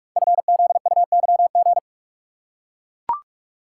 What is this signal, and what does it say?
Morse code, heard as a keyed tone, about 44 wpm, 700 Hz. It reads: FZUYG
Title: Code Groups - 5 Letter:Numbers - 45wpm